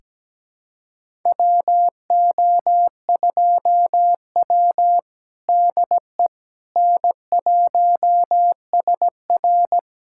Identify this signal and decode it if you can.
Morse code, heard as a keyed tone, about 17 wpm, 700 Hz. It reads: WO2W DE N1SR
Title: USA Callsign DE Callsign - 4 Characters - 17wpm